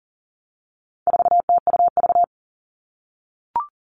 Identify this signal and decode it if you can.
Morse code, heard as a keyed tone, about 40 wpm, 700 Hz. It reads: 4TUV
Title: All Cut Numbers - 4-Digit 40wpm